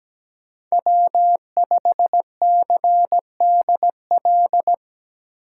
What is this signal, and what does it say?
Morse code, heard as a keyed tone, about 17 wpm, 700 Hz. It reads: W5CDL
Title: USA Call Signs - 5 Characters - 8x Character Spacing - 17wpm